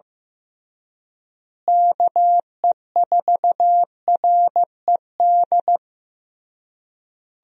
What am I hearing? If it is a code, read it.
Morse code, heard as a keyed tone, about 15 wpm, 700 Hz. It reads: KE4RED
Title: USA Call Signs - 6 Characters - Repeated 3x - 15wpm